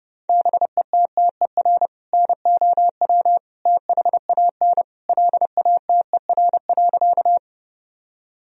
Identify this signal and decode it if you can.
Morse code, heard as a keyed tone, about 30 wpm, 700 Hz. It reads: BETTER NOW THAN LATER.
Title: Sentences from Top 1000 Words - Review Part 4 - 30wpm